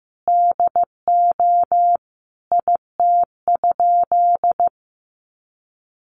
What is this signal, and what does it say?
Morse code, heard as a keyed tone, about 15 wpm, 700 Hz. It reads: DO IT?
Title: Sentences from Top 100 Words 15wpm